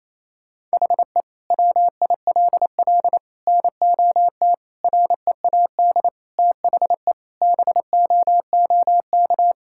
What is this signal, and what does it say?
Morse code, heard as a keyed tone, about 28 wpm, 700 Hz. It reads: HE WILL NOT READ THE BOOK
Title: Sentences from Top 1600 Words 28wpm